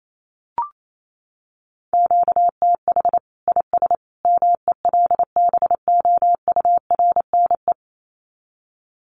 Morse code, heard as a keyed tone, about 28 wpm, 700 Hz. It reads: QTH IS MELBOURNE
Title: Common QSO Phrases 28wpm